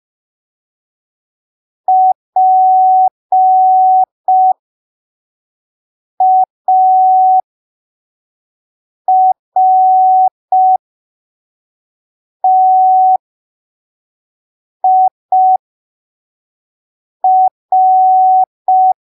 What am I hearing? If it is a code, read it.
Morse code, heard as a keyed tone, about 5 wpm, spaced as for 3 wpm, 750 Hz. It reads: PARTIR
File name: arquivos / texto / texto 15 05ppm